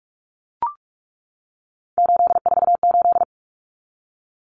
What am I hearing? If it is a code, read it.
Morse code, heard as a keyed tone, about 44 wpm, 700 Hz. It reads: 848
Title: Common Cut Numbers - 3-Digit 45wpm